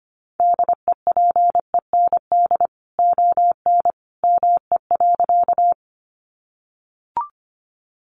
Morse code, heard as a keyed tone, about 25 wpm, 700 Hz. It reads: DEPEND ON ME.
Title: Sentences from Top 1300 Words 25wpm